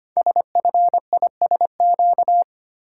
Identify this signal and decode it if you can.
Morse code, heard as a keyed tone, about 25 wpm, 700 Hz. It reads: SFISQ
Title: Code Groups - 5 Letter:Numbers - 25wpm